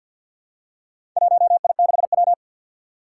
Morse code, heard as a keyed tone, about 53 wpm, 700 Hz. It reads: 1I6W
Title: Four Challenging Letter-Number Combinations 50wpm